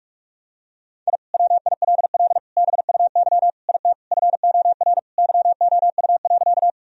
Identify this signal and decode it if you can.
Morse code, heard as a keyed tone, about 46 wpm, 700 Hz. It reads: I WILL BUY IT FOR YOU.
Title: Sentences from Top 1000 Words - Review Part 4 - 45wpm